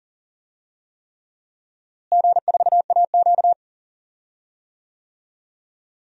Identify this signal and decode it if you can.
Morse code, heard as a keyed tone, about 37 wpm, 700 Hz. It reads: G4AQ
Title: DXCC-100 Call Signs - 4 Characters - Repeated 3x - 40wpm